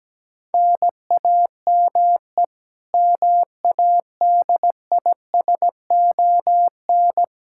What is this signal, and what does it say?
Morse code, heard as a keyed tone, about 17 wpm, 700 Hz. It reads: NAME MADISON
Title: Practice QSO - Exchange 1 - Beginning - 17wpm